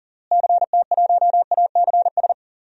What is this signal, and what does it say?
Morse code, heard as a keyed tone, about 40 wpm, 700 Hz. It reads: CT1ACS
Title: DXCC-100 Call Signs - 6 Characters - Repeated 2x - 40wpm